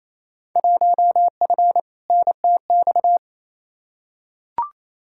Morse code, heard as a keyed tone, about 28 wpm, 700 Hz. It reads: 1F NTX
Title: Field Day Exchanges Encore - 28wpm